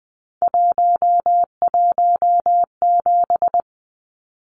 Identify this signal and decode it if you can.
Morse code, heard as a keyed tone, about 20 wpm, 700 Hz. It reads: 117
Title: Numbers - 3-Digit 20wpm